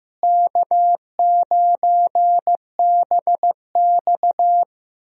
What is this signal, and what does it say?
Morse code, heard as a keyed tone, about 15 wpm, 700 Hz. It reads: K9BX
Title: USA Call Signs - 4 Characters - Repeated 2x - 15wpm